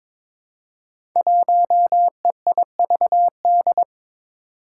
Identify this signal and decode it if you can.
Morse code, heard as a keyed tone, about 22 wpm, 700 Hz. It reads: 1EIVD
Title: Five Challenging Letter-Number Combinations 22wpm